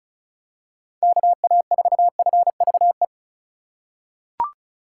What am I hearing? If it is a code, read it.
Morse code, heard as a keyed tone, about 34 wpm, 700 Hz. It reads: KA4FVE
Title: USA Call Signs - 6 Characters - 4x Character Spacing - 35wpm